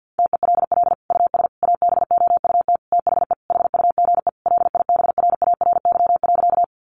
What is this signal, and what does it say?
Morse code, heard as a keyed tone, about 49 wpm, 700 Hz. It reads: TELL US ABOUT THE HUGE LIBRARY.
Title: Sentences from Top 1100 Words 50wpm